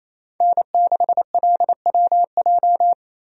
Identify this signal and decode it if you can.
Morse code, heard as a keyed tone, about 28 wpm, 700 Hz. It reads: N6LWJ
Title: USA Call Signs - 5 Characters - Repeated 3x - 28wpm